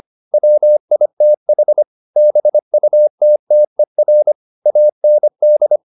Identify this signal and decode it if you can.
Morse code, heard as a keyed tone, about 25 wpm, 600 Hz. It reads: WITH BUTTER AND